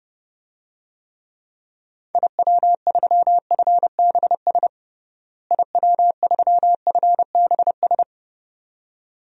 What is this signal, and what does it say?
Morse code, heard as a keyed tone, about 30 wpm, 700 Hz. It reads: IW3FBS IW3FBS
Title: DXCC-100 Call Signs - 6 Characters - Repeated 2x - 30wpm